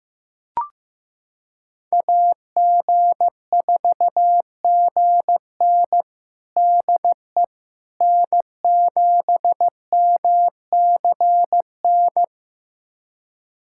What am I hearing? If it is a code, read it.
Morse code, heard as a keyed tone, about 15 wpm, 700 Hz. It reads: AG4GN DE N7MCN
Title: USA Callsign DE Callsign - 5 Characters - 15wpm